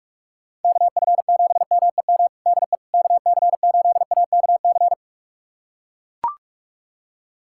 Morse code, heard as a keyed tone, about 41 wpm, 700 Hz. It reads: KF7MEM DE KC8AKC
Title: USA Callsign DE Callsign - 6 Characters - 45wpm